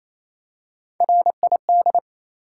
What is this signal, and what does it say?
Morse code, heard as a keyed tone, about 29 wpm, 700 Hz. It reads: RID
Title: Three Letter Words - 28wpm